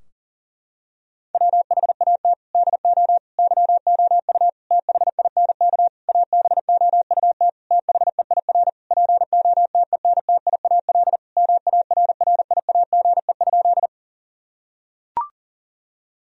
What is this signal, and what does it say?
Morse code, heard as a keyed tone, about 39 wpm, 700 Hz. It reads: WHAT DO YOU THINK ABOUT THEIR POTENTIAL MARRIAGE?
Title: Sentences from Top 1000 Words - Review Part 4 - 40wpm